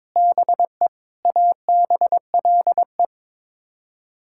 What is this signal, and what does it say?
Morse code, heard as a keyed tone, about 22 wpm, 700 Hz. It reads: BE ABLE
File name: practice / 22wpm / Build A Phrase - 5-Word Phrases - 22wpm